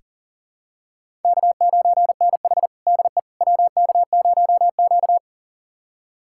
Morse code, heard as a keyed tone, about 40 wpm, 700 Hz. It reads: K9NH DE WK0Q
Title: USA Callsign DE Callsign - 4 Characters - 40wpm